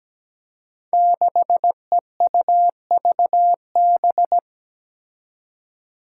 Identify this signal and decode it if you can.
Morse code, heard as a keyed tone, about 17 wpm, 700 Hz. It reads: 6EUVB
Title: Five Challenging Letter-Number Combinations 17wpm